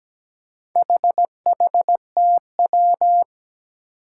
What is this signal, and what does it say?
Morse code, heard as a keyed tone, about 17 wpm, 700 Hz. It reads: HHTW
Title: Four Challenging Letter-Number Combinations 17wpm